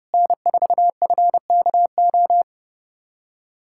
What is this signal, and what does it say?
Morse code, heard as a keyed tone, about 31 wpm, 700 Hz. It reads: N4FKO
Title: USA Callsign DE Callsign - 5 Characters - 30wpm